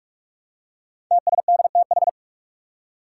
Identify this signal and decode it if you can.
Morse code, heard as a keyed tone, about 46 wpm, 700 Hz. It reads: TSDTH
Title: Five Challenging Letter-Number Combinations 45wpm